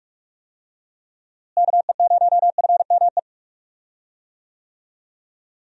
Morse code, heard as a keyed tone, about 46 wpm, 700 Hz. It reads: KE0FME
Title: USA Call Signs - 6 Characters - Repeated 3x - 45wpm